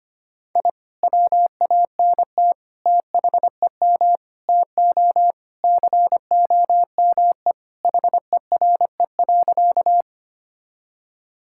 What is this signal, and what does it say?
Morse code, heard as a keyed tone, about 25 wpm, 700 Hz. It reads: I WANT THEM TO COME HERE.
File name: practice / 25wpm / Sentences with 2-Word Phrases - 25wpm